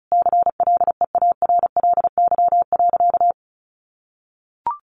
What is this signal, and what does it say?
Morse code, heard as a keyed tone, about 34 wpm, 700 Hz. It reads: CLEARLY.
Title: Sentences from Top 1300 Words 35wpm